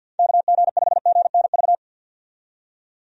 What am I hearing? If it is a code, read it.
Morse code, heard as a keyed tone, about 51 wpm, 700 Hz. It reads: KG5GNV
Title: USA Call Signs - 6 Characters - 50wpm